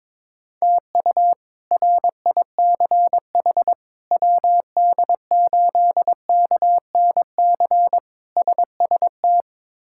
Morse code, heard as a keyed tone, about 22 wpm, 700 Hz. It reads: TU RICH WD8KNC SST
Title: K1USN SST QSO Practice - 22wpm